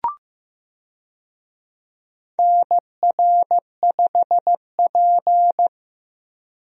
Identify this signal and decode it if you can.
Morse code, heard as a keyed tone, about 15 wpm, 700 Hz. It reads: NR5P
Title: USA Call Signs - 4 Characters - Repeated 3x - 15wpm